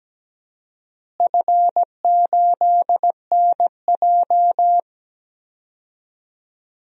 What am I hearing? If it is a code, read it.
Morse code, heard as a keyed tone, about 17 wpm, 700 Hz. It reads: F8NJ
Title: DXCC-100 Call Signs - 4 Characters - Repeated 3x - 17wpm